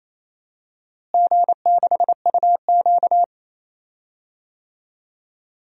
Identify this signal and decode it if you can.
Morse code, heard as a keyed tone, about 28 wpm, 700 Hz. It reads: G6UQ
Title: DXCC-100 Call Signs - 4 Characters - Repeated 3x - 28wpm